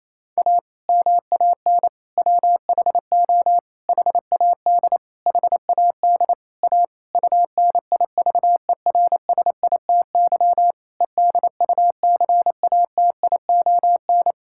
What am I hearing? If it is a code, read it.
Morse code, heard as a keyed tone, about 27 wpm, 700 Hz. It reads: A MAN WHO HAD HAD A UNIVERSITY EDUCATION
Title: Sentences from Top 1000 Words - Review Part 6 - 28wpm